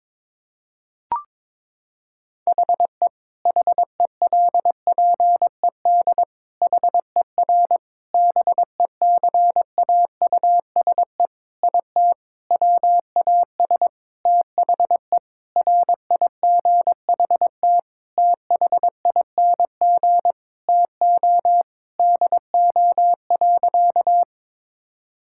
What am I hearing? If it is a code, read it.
Morse code, heard as a keyed tone, about 22 wpm, 700 Hz. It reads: HE HELPED HER BECAUSE IT WAS THE RIGHT THING TO DO.
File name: practice / 22wpm / Sentences with 8-Word Phrases - 22wpm